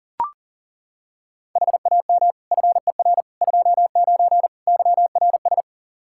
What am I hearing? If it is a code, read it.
Morse code, heard as a keyed tone, about 37 wpm, 700 Hz. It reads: HAM FER 29 YRS
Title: K1USN SST QSO Practice 40wpm